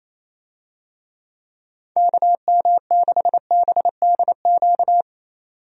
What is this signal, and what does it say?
Morse code, heard as a keyed tone, about 28 wpm, 700 Hz. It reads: KM6BDQ
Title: USA Call Signs - 6 Characters - 28wpm